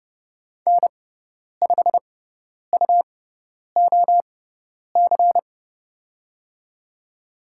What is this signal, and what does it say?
Morse code, heard as a keyed tone, about 30 wpm, spaced as for 10 wpm, 690 Hz. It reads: N5UOC
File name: Lesson 18g - Letter m - Call Signs 30wpm